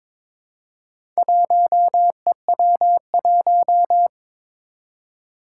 Morse code, heard as a keyed tone, about 22 wpm, 700 Hz. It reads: 1EW1
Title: Four Challenging Letter-Number Combinations 22wpm